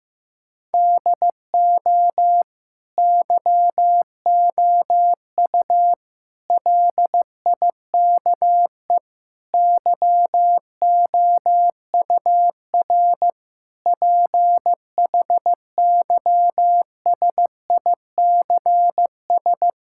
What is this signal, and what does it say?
Morse code, heard as a keyed tone, about 16 wpm, 700 Hz. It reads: DO YOU LIKE YOUR PHYSICS
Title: Sentences from Top 1000 Words - Review Part 5 - 15wpm